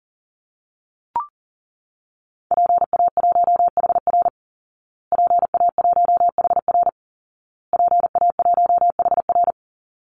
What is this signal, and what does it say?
Morse code, heard as a keyed tone, about 37 wpm, 700 Hz. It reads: PA1HR PA1HR PA1HR
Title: Call Signs Repeated 2x - 40wpm